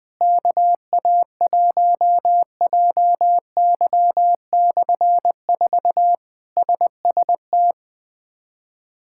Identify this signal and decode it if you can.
Morse code, heard as a keyed tone, about 20 wpm, 700 Hz. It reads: KA1JY/4 SST
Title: K1USN SST QSO Practice - 20wpm